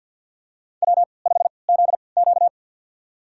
Morse code, heard as a keyed tone, about 52 wpm, spaced as for 29 wpm, 700 Hz. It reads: W5ZQ
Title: USA Call Signs - 4 Characters - 3x Character Spacing - 50wpm